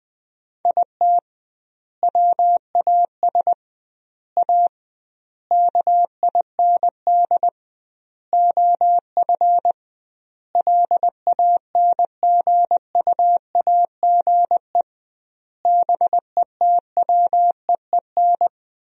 Morse code, heard as a keyed tone, about 20 wpm, 700 Hz. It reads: IT WAS A KIND OF LANGUAGE BETWEEN
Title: 2x Word-Spacing - Sentences from Top 500 Words - 20wpm